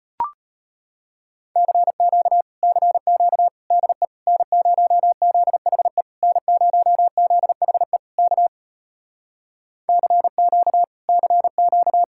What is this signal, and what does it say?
Morse code, heard as a keyed tone, about 33 wpm, 700 Hz. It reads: CQ CQ DE N0ZHE N0ZHE K CQ CQ